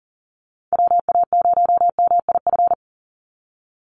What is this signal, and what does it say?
Morse code, heard as a keyed tone, about 40 wpm, 700 Hz. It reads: WA0MIF
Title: USA Call Signs 40wpm